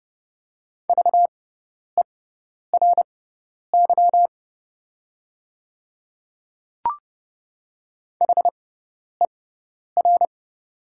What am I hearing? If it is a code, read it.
Morse code, heard as a keyed tone, about 30 wpm, spaced as for 10 wpm, 700 Hz. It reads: VERY HER
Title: 6x Character Spacing - Top 200 Words - 30wpm